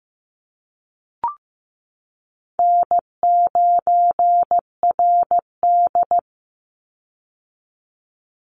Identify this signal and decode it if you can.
Morse code, heard as a keyed tone, about 15 wpm, 700 Hz. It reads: N9RD
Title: Easy Call Signs Repeated 2x - 15wpm